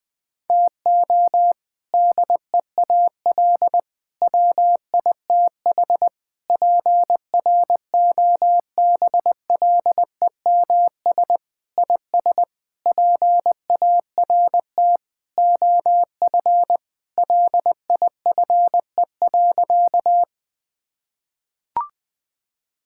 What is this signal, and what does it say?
Morse code, heard as a keyed tone, about 21 wpm, 700 Hz. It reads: TO DEAL WITH PROBLEMS IS PART OF LIFE.
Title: Sentences with 3-Word Phrases - 20wpm